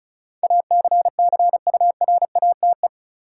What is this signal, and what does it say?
Morse code, heard as a keyed tone, about 36 wpm, 700 Hz. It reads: ACCURATE
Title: Sentences from Top 1200 Words - New Word Drill - 35wpm